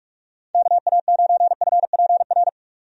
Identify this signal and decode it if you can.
Morse code, heard as a keyed tone, about 44 wpm, 700 Hz. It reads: KA9FPR
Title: USA Call Signs - 6 Characters - 45wpm